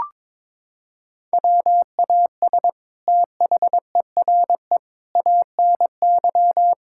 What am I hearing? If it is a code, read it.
Morse code, heard as a keyed tone, about 22 wpm, 700 Hz. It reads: WAS THERE ANY
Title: Sentences from Top 1000 Words - Review Part 6 - 22wpm